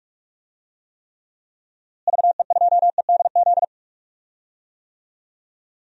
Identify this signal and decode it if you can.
Morse code, heard as a keyed tone, about 42 wpm, 700 Hz. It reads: VE2EDZ
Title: DXCC-100 Call Signs - 6 Characters - Repeated 3x - 45wpm